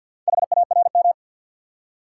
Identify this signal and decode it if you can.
Morse code, heard as a keyed tone, about 52 wpm, 700 Hz. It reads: HARM
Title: Four Letter Words - 50wpm